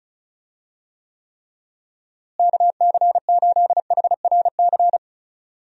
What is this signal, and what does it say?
Morse code, heard as a keyed tone, about 36 wpm, 700 Hz. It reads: KC8HRC
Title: USA Call Signs - 6 Characters - Repeated 3x - 35wpm